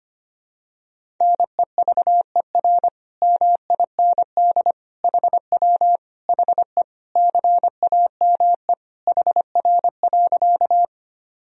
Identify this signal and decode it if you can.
Morse code, heard as a keyed tone, about 25 wpm, 700 Hz. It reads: NEVER MIND HW HE CAME HR.
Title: Sentences from Top 500 Words and Abbreviations 25wpm